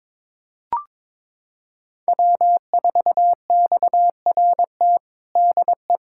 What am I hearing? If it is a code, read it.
Morse code, heard as a keyed tone, about 21 wpm, 700 Hz. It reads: W4XRT DE
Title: USA Callsign DE Callsign 22wpm